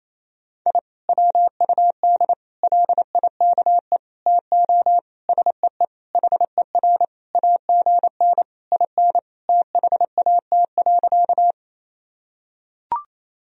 Morse code, heard as a keyed tone, about 27 wpm, 700 Hz. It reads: I WUD LIKE TO SEE HER AGN IN THAT.
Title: Sentences from Top 200 Words and Abbreviations 28wpm